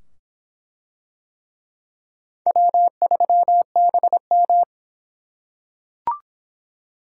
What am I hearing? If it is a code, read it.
Morse code, heard as a keyed tone, about 23 wpm, 700 Hz. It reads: W3BM